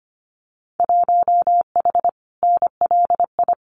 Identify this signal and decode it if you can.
Morse code, heard as a keyed tone, about 25 wpm, 700 Hz. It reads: 1H NLI
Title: Winter Field Day Exchanges 25wpm